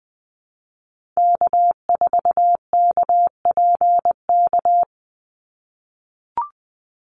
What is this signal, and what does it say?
Morse code, heard as a keyed tone, about 20 wpm, 700 Hz. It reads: K4KPK
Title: NA SOTA Activator Callsigns 20wpm